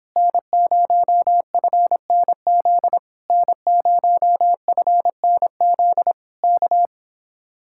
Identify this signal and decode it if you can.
Morse code, heard as a keyed tone, about 26 wpm, 700 Hz. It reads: N0FNZ N0FNZ K